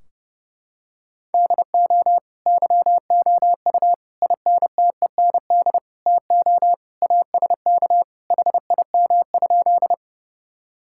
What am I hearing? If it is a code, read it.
Morse code, heard as a keyed tone, about 30 wpm, 700 Hz. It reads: DO YOU INTEND TO ASK HIM?
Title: Sentences from Top 1600 Words 30wpm